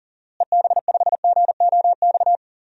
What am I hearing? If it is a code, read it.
Morse code, heard as a keyed tone, about 37 wpm, 700 Hz. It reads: EB5GOX
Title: DXCC-100 Call Signs - 6 Characters - Repeated 3x - 40wpm